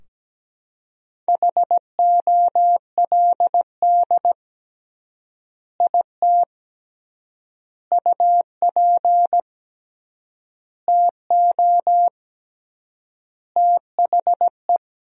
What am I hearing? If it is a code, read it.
Morse code, heard as a keyed tone, about 17 wpm, 700 Hz. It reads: HOLD IT UP TO THE
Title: 3x Word-Spacing - Sentences from Top 400 Words - 17wpm